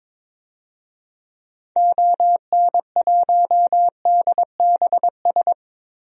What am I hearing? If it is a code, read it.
Morse code, heard as a keyed tone, about 22 wpm, 700 Hz. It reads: ON1DBS
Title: DXCC-100 Call Signs - 6 Characters - Repeated 2x - 22wpm